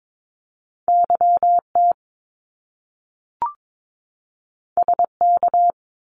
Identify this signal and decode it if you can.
Morse code, heard as a keyed tone, about 22 wpm, 700 Hz. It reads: YT SK
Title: Canadian Provinces and Territories 22wpm